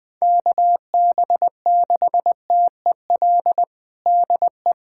Morse code, heard as a keyed tone, about 20 wpm, 700 Hz. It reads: KB6TEL DE
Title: USA Callsign DE Callsign 20wpm